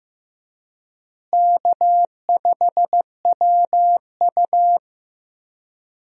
Morse code, heard as a keyed tone, about 16 wpm, 700 Hz. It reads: K5WU
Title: USA Call Signs - 4 Characters - Repeated 3x - 15wpm